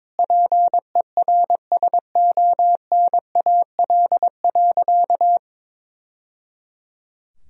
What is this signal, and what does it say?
Morse code, heard as a keyed tone, about 22 wpm, 700 Hz. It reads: PERSONAL.
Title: Sentences from Top 1000 Words - Review Part 4 - 22wpm